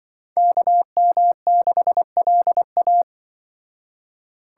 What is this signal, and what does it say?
Morse code, heard as a keyed tone, about 25 wpm, 700 Hz. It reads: KM6LA
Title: Speed Racing - USA 2x2 Call Signs 20wpm